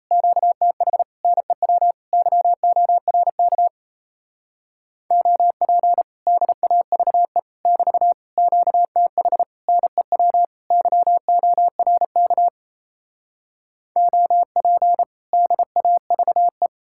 Morse code, heard as a keyed tone, about 38 wpm, 700 Hz. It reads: QTH NEW YORK OP DAVE <BT> QTH NEW YORK OP DAVE